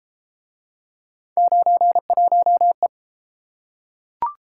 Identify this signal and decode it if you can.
Morse code, heard as a keyed tone, about 33 wpm, 700 Hz. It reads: 91E